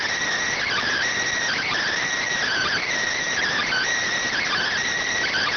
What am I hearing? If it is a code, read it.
Emergency Alert System (SAME) tones, heard as harsh buzzing bursts. It is heard from 0.5 to 4.7 s.